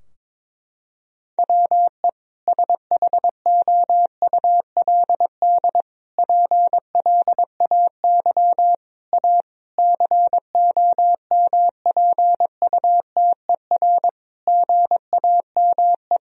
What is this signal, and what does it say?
Morse code, heard as a keyed tone, about 22 wpm, 700 Hz. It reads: WE SHOULD PLAY A COMPUTER GAME
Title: Sentences from Top 1000 Words - Review Part 3 - 22wpm